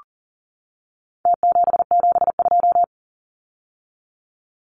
Morse code, heard as a keyed tone, about 37 wpm, 700 Hz. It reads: T772
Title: Numbers - Contest Style - 4-Digit 40wpm